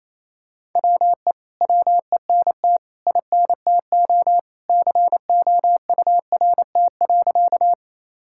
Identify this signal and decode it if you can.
Morse code, heard as a keyed tone, about 28 wpm, 700 Hz. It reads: WE WENT INTO COURT.
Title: Sentences from Top 1600 Words 28wpm